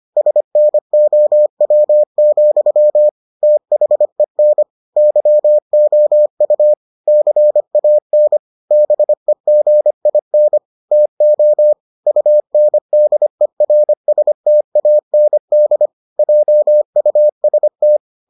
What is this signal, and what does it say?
Morse code, heard as a keyed tone, about 25 wpm, 600 Hz. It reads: SNOW, THEN YOU CAN BEGIN TO UNDERSTAND JUST